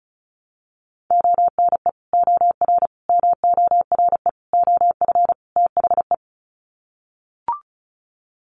Morse code, heard as a keyed tone, about 34 wpm, 700 Hz. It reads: ONE OR MORE OF THE
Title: Sets of 5 Words Encore 35wpm